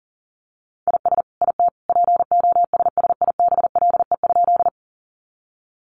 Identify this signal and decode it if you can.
Morse code, heard as a keyed tone, about 40 wpm, 700 Hz. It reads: IS IT POSSIBLE?
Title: Sentences from Top 500 Words 40wpm